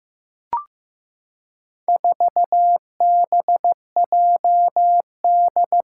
Morse code, heard as a keyed tone, about 15 wpm, 700 Hz. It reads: 4BJD
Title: Four Challenging Letter-Number Combinations 15wpm